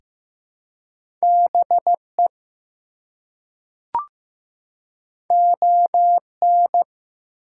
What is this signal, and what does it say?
Morse code, heard as a keyed tone, about 15 wpm, 700 Hz. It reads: BE ON
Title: Two Letter Words - 15wpm